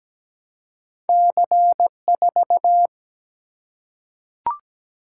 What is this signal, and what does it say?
Morse code, heard as a keyed tone, about 17 wpm, 700 Hz. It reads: C4
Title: Single-Letter Single-Number 17wpm